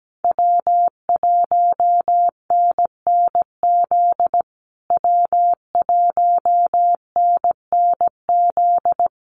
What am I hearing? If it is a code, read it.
Morse code, heard as a keyed tone, about 17 wpm, 700 Hz. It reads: W1NNZ W1NNZ
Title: USA Call Signs Repeated 3x 17wpm